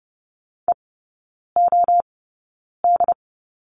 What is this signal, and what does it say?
Morse code, heard as a keyed tone, about 30 wpm, spaced as for 9 wpm, 700 Hz. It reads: EOD
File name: Lesson 18e - Letter m - Three Characters 30wpm